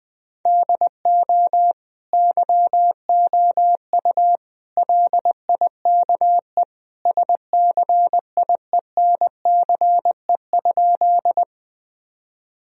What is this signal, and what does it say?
Morse code, heard as a keyed tone, about 20 wpm, 700 Hz. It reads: DO YOU LIKE SCIENCE?
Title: Sentences from Top 1000 Words - Review Part 2 - 20wpm